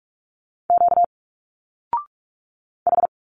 Morse code, heard as a keyed tone, about 45 wpm, 700 Hz. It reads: Q H
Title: Single Letters 45wpm